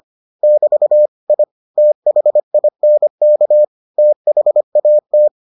Morse code, heard as a keyed tone, about 25 wpm, 600 Hz. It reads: <BT> I THINK THAT